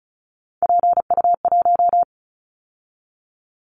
Morse code, heard as a keyed tone, about 33 wpm, 700 Hz. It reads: PU1
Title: Callsign Prefixes for Top 50 DXCC - 35wpm